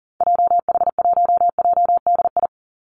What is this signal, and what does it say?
Morse code, heard as a keyed tone, about 40 wpm, 700 Hz. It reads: JH1JDI
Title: Call Signs Repeated 2x - 40wpm